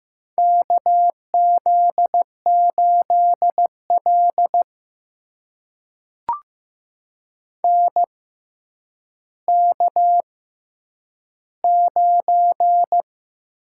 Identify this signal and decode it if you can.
Morse code, heard as a keyed tone, about 15 wpm, spaced as for 9 wpm, 700 Hz. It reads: KZ8L NK9
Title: USA Call Signs - 4 Characters - 6x Character Spacing - 15wpm